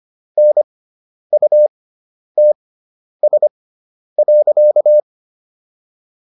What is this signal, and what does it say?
Morse code, heard as a keyed tone, about 25 wpm, spaced as for 10 wpm, 600 Hz. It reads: NUTS.